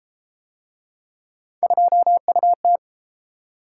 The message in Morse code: 2UT

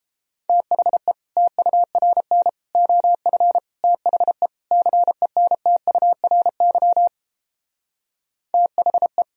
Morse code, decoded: THE TURN OF THE CENTURY THE

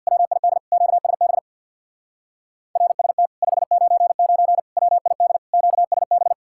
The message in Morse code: WID QSB RST 599 WID QSB

